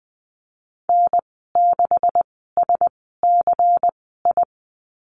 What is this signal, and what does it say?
Morse code, heard as a keyed tone, about 20 wpm, spaced as for 14 wpm, 700 Hz. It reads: N6SCI
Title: 245 Lesson 39g - Letter z - Call Signs 20wpm